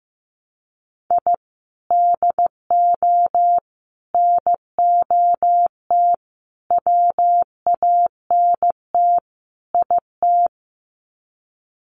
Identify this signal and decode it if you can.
Morse code, heard as a keyed tone, about 15 wpm, 700 Hz. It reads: I DO NOT WANT IT
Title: Sets of 5 Words Encore2 15wpm